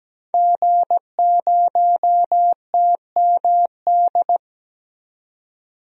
Morse code, heard as a keyed tone, about 17 wpm, 700 Hz. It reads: G0TMD
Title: DXCC-100 Call Signs - 5 Characters - 17wpm